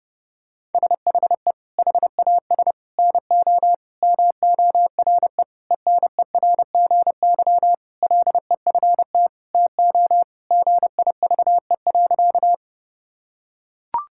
Morse code, heard as a keyed tone, about 30 wpm, 700 Hz. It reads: SHE HAS NO MORE ENERGY LEFT TO GIVE.
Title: Sentences from Top 1000 Words - Review Part 6 - 30wpm